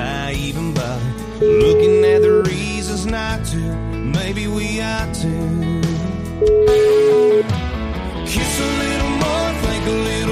Ringback tone (Europe, 425 Hz), with noise.